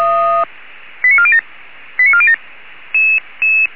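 Five-tone selective call (ZVEI), heard as quick runs of five tones, tones of about 80 ms. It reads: 89407 89407
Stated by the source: Five-tone ZVEI selective call received on SDR.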